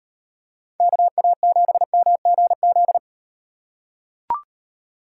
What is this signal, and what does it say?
Morse code, heard as a keyed tone, about 38 wpm, 700 Hz. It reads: KA7MGZ